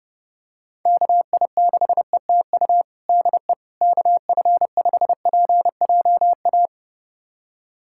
Morse code, heard as a keyed tone, about 30 wpm, 700 Hz. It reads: KI6ETU DE KF5PJA
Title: USA Callsign DE Callsign - 6 Characters - 30wpm